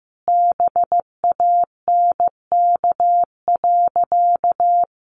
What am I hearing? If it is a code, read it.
Morse code, heard as a keyed tone, about 15 wpm, 700 Hz. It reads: BANK.
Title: Sentences from Top 1400 Words 15wpm